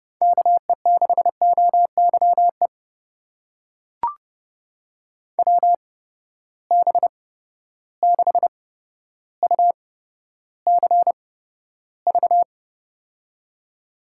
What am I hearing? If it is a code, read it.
Morse code, heard as a keyed tone, about 32 wpm, spaced as for 13 wpm, 700 Hz. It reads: KE6OYE WB6UCV
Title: USA Call Signs - 6 Characters - 8x Character Spacing - 30wpm